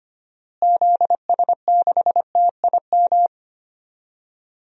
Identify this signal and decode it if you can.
Morse code, heard as a keyed tone, about 25 wpm, 700 Hz. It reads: ZS6TIM
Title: DXCC-100 Call Signs - 6 Characters - Repeated 2x - 25wpm